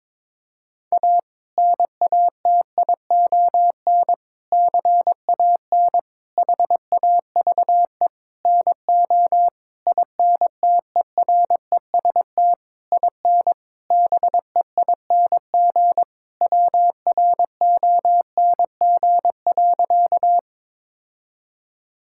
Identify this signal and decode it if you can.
Morse code, heard as a keyed tone, about 22 wpm, 700 Hz. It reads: A NATION CAN HAVE NO INTEREST IN BEING WRONG.
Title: Sentences from Top 1000 Words - Review Part 6 - 22wpm